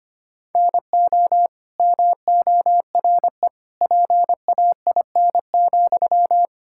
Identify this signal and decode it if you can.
Morse code, heard as a keyed tone, about 25 wpm, 700 Hz. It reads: NO MORE PAIN,
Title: Sentences from Top 1600 Words 25wpm